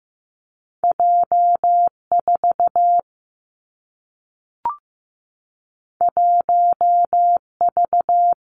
Morse code, heard as a keyed tone, about 15 wpm, 700 Hz. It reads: J 4 1 V
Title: Two Challenging Letter-Number Combinations 15wpm